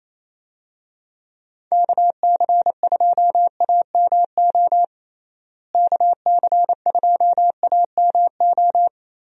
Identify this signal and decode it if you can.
Morse code, heard as a keyed tone, about 27 wpm, 700 Hz. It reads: KC2AMO KC2AMO
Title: USA Call Signs - 6 Characters - Repeated 2x - 28wpm